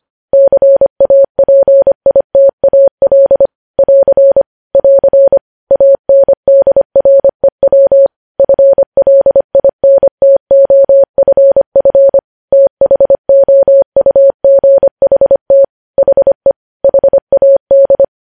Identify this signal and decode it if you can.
Morse code, heard as a keyed tone, about 25 wpm, 550 Hz. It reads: CAPITAL <AR> <AR> ANDREW FLINTOFF THOUGHT HE HAD
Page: News Headlines in Morse Code at 25 WPM